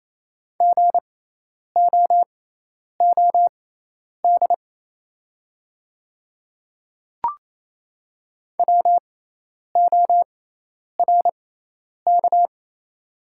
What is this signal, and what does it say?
Morse code, heard as a keyed tone, about 28 wpm, spaced as for 10 wpm, 700 Hz. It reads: GOOD WORK
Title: 6x Character Spacing - Top 200 Words - 28wpm